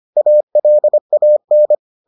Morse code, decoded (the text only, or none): ALAN